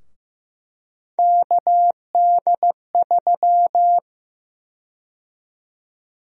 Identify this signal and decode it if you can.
Morse code, heard as a keyed tone, about 15 wpm, 700 Hz. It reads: KD3
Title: USA Call Sign Prefixes - 15wpm